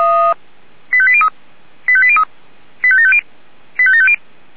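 Five-tone selective call (ZVEI), heard as quick runs of five tones, tones of about 70 ms. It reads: 85803 85803 85750 85750
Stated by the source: ZVEI five-tone selective call 85726, received on SDR.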